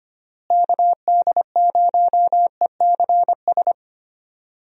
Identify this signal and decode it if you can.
Morse code, heard as a keyed tone, about 25 wpm, 700 Hz. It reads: KD0ECS
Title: USA Call Signs - 6 Characters - Repeated 3x - 25wpm